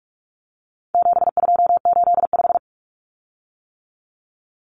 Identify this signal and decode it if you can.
Morse code, heard as a keyed tone, about 44 wpm, 700 Hz. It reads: 7285
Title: Common Cut Numbers - 4-Digit 45wpm